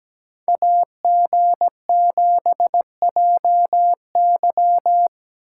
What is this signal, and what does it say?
Morse code, heard as a keyed tone, about 20 wpm, 700 Hz. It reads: AG7JY
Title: Speed Racing - USA 2x2 Call Signs 17wpm